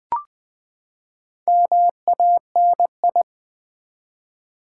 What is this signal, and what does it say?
Morse code, heard as a keyed tone, about 20 wpm, 700 Hz. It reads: MANI
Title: Build A Word - 4 Letters At A Time - 12-Letter Words - 20wpm